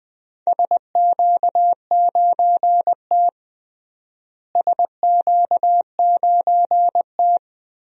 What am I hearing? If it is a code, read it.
Morse code, heard as a keyed tone, about 20 wpm, 700 Hz. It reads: SQ9T SQ9T
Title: DXCC-100 Call Signs - 4 Characters - Repeated 3x - 20wpm